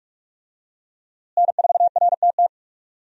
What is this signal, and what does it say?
Morse code, heard as a keyed tone, about 46 wpm, 700 Hz. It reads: N4RTT
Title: USA Call Signs - 5 Characters - Repeated 2x - 45wpm